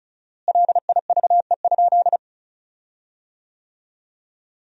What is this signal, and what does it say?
Morse code, heard as a keyed tone, about 36 wpm, 700 Hz. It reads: LIVE?
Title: 4x Word-Spacing - Sentences from Top 200 Words - 35wpm